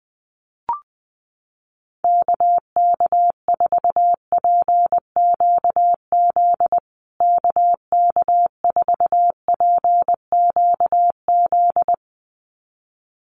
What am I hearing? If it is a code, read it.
Morse code, heard as a keyed tone, about 20 wpm, 700 Hz. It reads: KK4PQZ KK4PQZ
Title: USA Call Signs Repeated 2x 20wpm